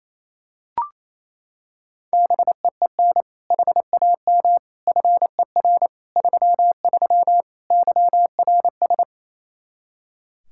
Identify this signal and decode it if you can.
Morse code, heard as a keyed tone, about 28 wpm, 700 Hz. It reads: BEEN HAM FER 33 YRS
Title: Practice QSO - Exchange 3 - More Info - 28wpm